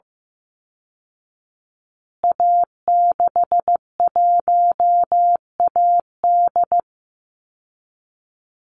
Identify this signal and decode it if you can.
Morse code, heard as a keyed tone, about 15 wpm, 700 Hz. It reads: A61AD
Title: Call Signs Repeated 2x - 15wpm